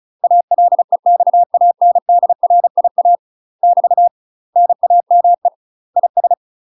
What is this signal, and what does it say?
Morse code, heard as a keyed tone, about 34 wpm, 700 Hz. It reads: ALEXANDRIA <BT> NAME IS
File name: UK MASTER QSOs TRAINING/35WPM